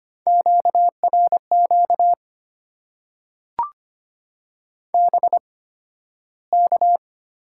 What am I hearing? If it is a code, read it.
Morse code, heard as a keyed tone, about 25 wpm, spaced as for 13 wpm, 700 Hz. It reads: QRQ BK
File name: Top 90 QSO Elements - 8x Character Spacing - 25wpm